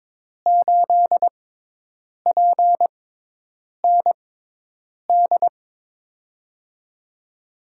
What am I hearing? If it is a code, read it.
Morse code, heard as a keyed tone, about 22 wpm, spaced as for 8 wpm, 700 Hz. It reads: 8PND
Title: USA Call Signs - 6 Characters - 6x Character Spacing - 22wpm